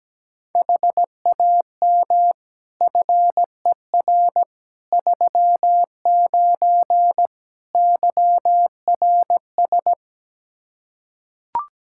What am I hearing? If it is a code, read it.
Morse code, heard as a keyed tone, about 17 wpm, 700 Hz. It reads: HAM FER 39 YRS
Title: Practice QSO - Exchange 3 - More Info - 17wpm